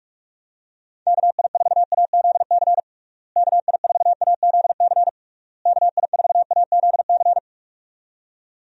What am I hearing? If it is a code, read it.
Morse code, heard as a keyed tone, about 46 wpm, 700 Hz. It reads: KI4AZC KI4AZC KI4AZC
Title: USA Call Signs - 6 Characters - Repeated 3x - 45wpm